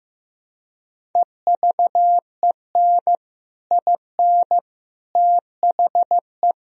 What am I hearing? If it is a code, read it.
Morse code, heard as a keyed tone, about 15 wpm, 700 Hz. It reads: EVEN IN THE
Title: Sentences from Top 1000 Words - Review Part 1 - 15wpm